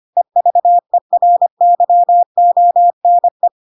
Morse code, heard as a keyed tone, about 25 wpm, 700 Hz. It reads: EVERYONE
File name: Pinocchio / Pinocchio ch1 25 25 ik1hge